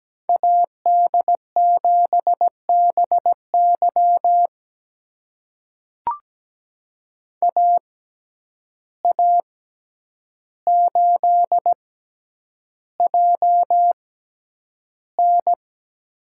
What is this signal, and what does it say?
Morse code, heard as a keyed tone, about 17 wpm, spaced as for 9 wpm, 700 Hz. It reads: AD7BY AA8JN
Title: USA Call Signs - 5 Characters - 6x Character Spacing - 17wpm